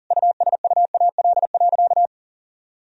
Morse code, heard as a keyed tone, about 40 wpm, 700 Hz. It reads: USUAL.
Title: Sentences from Top 1000 Words - Review Part 2 - 40wpm